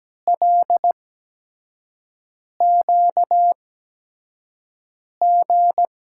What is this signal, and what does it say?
Morse code, heard as a keyed tone, about 17 wpm, 700 Hz. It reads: L Q G
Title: USA Call Signs - 6 Characters - 8x Character Spacing - 17wpm